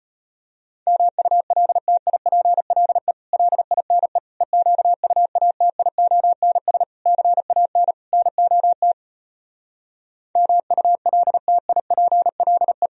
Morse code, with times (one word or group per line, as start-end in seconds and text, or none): 0.9-3.1 MULTIPLE
3.3-4.2 LINE
4.4-6.8 EQUATIONS
7.1-7.9 CAN
8.1-8.9 NOT
10.4-12.9 MULTIPLE